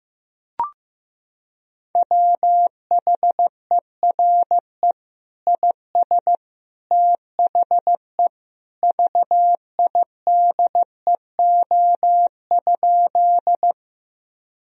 Morse code, heard as a keyed tone, about 15 wpm, 700 Hz. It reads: WHERE IS THE VIDEO?
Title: Sentences from Top 1000 Words - Review Part 4 - 15wpm